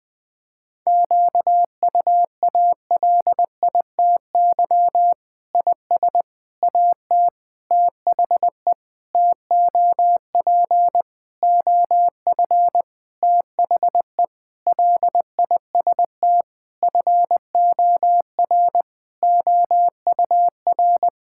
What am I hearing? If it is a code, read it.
Morse code, heard as a keyed tone, about 21 wpm, 700 Hz. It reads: QUALITY IS AT THE TOP OF THE LIST FOR OUR
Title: Sentences with 6-Word Phrases - Spoken Beginning - 20wpm